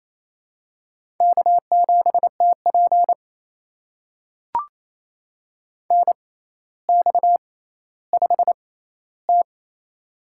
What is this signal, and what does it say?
Morse code, heard as a keyed tone, about 27 wpm, spaced as for 14 wpm, 700 Hz. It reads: K7TP NX5T
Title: USA Call Signs - 4 Characters - 6x Character Spacing - 28wpm